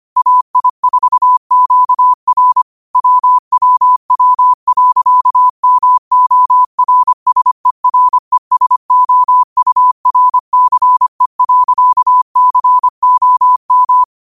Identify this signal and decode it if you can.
Morse code, heard as a keyed tone, about 25 wpm, 1000 Hz. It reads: AI4QR WWW.MORSERESOURCE.COM
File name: qotdfiles / 25wpm